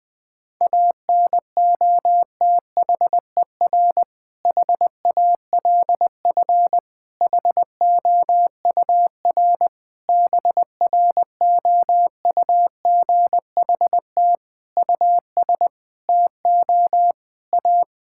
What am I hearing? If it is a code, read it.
Morse code, heard as a keyed tone, about 21 wpm, 700 Hz. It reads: ANOTHER HALF HOUR BROUGHT US TO A
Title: Sentences from Top 1600 Words 20wpm